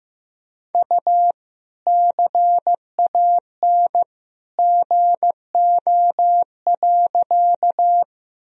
Morse code, heard as a keyed tone, about 15 wpm, 700 Hz. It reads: U CAN GO.
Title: Sentences from Top 100 Words and Abbreviations 15wpm